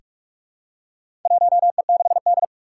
Morse code, heard as a keyed tone, about 43 wpm, 700 Hz. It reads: 1E6D
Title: Four Challenging Letter-Number Combinations 45wpm